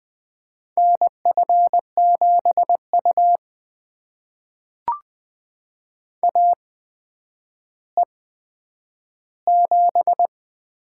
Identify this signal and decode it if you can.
Morse code, heard as a keyed tone, about 20 wpm, spaced as for 10 wpm, 700 Hz. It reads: NF7U AE7
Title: USA Call Signs - 8x Character Spacing - 20wpm